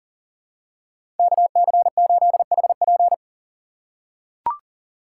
Morse code, heard as a keyed tone, about 40 wpm, 700 Hz. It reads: KC8HP
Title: USA Call Signs - 5 Characters - 4x Character Spacing - 40wpm